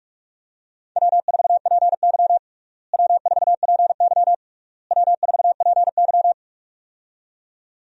Morse code, heard as a keyed tone, about 41 wpm, 700 Hz. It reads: W4PY W4PY W4PY
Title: USA Call Signs - 4 Characters - Repeated 3x - 45wpm